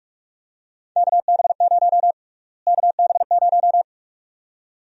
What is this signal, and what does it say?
Morse code, heard as a keyed tone, about 46 wpm, 700 Hz. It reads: KB0 KB0
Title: USA Call Sign Prefixes - Repeated 2x - 45wpm